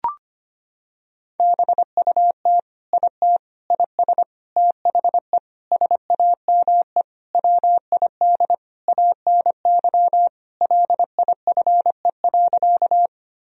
Morse code, heard as a keyed tone, about 25 wpm, 700 Hz. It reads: BUT IT IS THE SAME WID ANY LIFE.
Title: Sentences from Top 300 Words and Abbreviations 25wpm